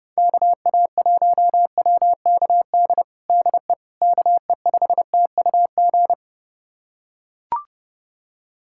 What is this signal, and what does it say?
Morse code, heard as a keyed tone, about 30 wpm, 700 Hz. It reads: KA1WKD DE KE5TUG
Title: USA Callsign DE Callsign - 6 Characters - 30wpm